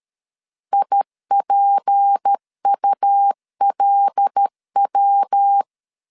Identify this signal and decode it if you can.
Morse code, heard as a keyed tone, about 13 wpm, 800 Hz. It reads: IPULW
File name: LEZIONI / 6 PKX0 / 6 PKX0 12wpm 2231